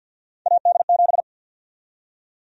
Morse code, heard as a keyed tone, about 53 wpm, 700 Hz. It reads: AD7
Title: USA Call Sign Prefixes - 50wpm